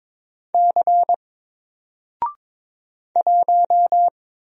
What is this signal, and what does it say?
Morse code, heard as a keyed tone, about 22 wpm, 700 Hz. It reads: C 1
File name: Single-Letter-Number-22wpm